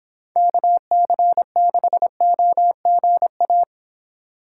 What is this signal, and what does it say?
Morse code, heard as a keyed tone, about 26 wpm, 700 Hz. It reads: KC6OGA